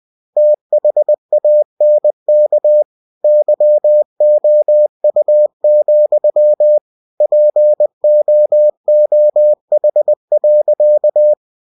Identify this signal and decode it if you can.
Morse code, heard as a keyed tone, about 20 wpm, 600 Hz. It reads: THANK YOU, POOH.
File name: cw-books / winnie-the-pooh / winnie-20wpm-page-0076